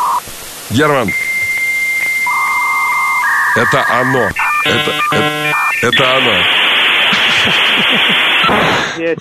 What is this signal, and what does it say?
Dial-up modem handshake, with speech.